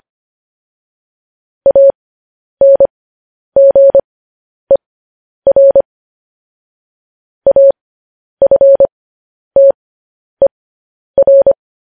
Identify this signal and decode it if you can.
Morse code, heard as a keyed tone, about 25 wpm, spaced as for 10 wpm, 550 Hz. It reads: ANGER AFTER
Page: news headlines in morse code at 10 wpm